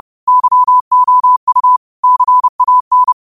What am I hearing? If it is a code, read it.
Morse code, heard as a keyed tone, about 30 wpm, 1000 Hz. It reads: YOU CAN
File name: qotdfiles / 30wpm